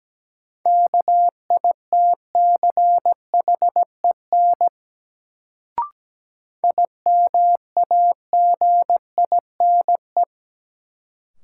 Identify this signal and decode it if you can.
Morse code, heard as a keyed tone, about 17 wpm, 700 Hz. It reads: KITCHEN IMAGINE
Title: Sentences from Top 1400 Words - New Word Drill Alt - 17wpm